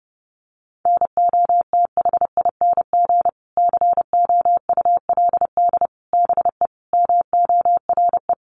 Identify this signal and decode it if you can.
Morse code, heard as a keyed tone, about 30 wpm, 700 Hz. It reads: NOTHING COULD BE MORE
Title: Sentences from Top 1200 Words 30wpm